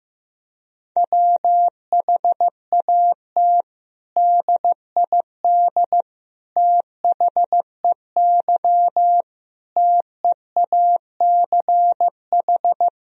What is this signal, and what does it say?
Morse code, heard as a keyed tone, about 15 wpm, 700 Hz. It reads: WHAT DID THEY TEACH
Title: Sentences from Top 1000 Words - Review Part 2 - 15wpm